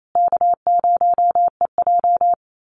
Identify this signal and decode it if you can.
Morse code, heard as a keyed tone, about 28 wpm, 700 Hz. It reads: K0EJ
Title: Call Signs Repeated 2x - 28wpm